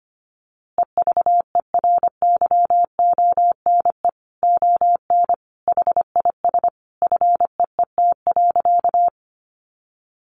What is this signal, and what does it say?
Morse code, heard as a keyed tone, about 25 wpm, 700 Hz. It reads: EVERYONE ON HIS FEET.
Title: A Sound Of Thunder 25wpm